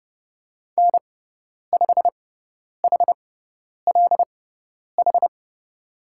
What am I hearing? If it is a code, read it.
Morse code, heard as a keyed tone, about 30 wpm, spaced as for 10 wpm, 700 Hz. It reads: N5HLH
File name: Lesson 18g - Letter m - Call Signs 30wpm